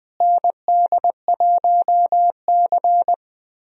Morse code, heard as a keyed tone, about 20 wpm, 700 Hz. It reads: ND1C
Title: USA Call Signs - 4 Characters - Repeated 3x - 20wpm